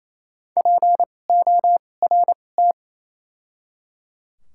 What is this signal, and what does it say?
Morse code, heard as a keyed tone, about 28 wpm, spaced as for 20 wpm, 700 Hz. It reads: PORT
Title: 2x Character Spacing - Top 500 Words - 28wpm